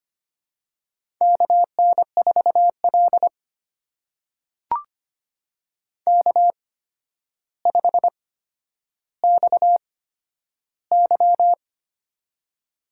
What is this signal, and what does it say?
Morse code, heard as a keyed tone, about 25 wpm, spaced as for 11 wpm, 700 Hz. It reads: KN4L K5XY
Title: USA Call Signs - 4 Characters - 8x Character Spacing - 25wpm